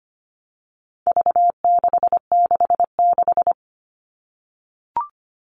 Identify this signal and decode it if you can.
Morse code, heard as a keyed tone, about 25 wpm, 700 Hz. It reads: V666
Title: All Cut Numbers - 4-Digit 25wpm